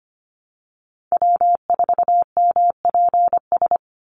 Morse code, heard as a keyed tone, about 25 wpm, 700 Hz. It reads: W4MPS
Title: NA SOTA Chaser Callsigns 25wpm